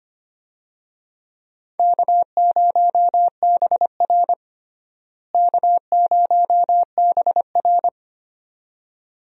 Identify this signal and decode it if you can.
Morse code, heard as a keyed tone, about 25 wpm, 700 Hz. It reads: K0BR K0BR
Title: USA Call Signs - 4 Characters - Repeated 2x - 25wpm